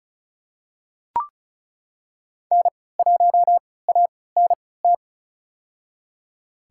Morse code, heard as a keyed tone, about 34 wpm, spaced as for 20 wpm, 700 Hz. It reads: N1ANT
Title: USA Call Signs - 5 Characters - 3x Character Spacing - 35wpm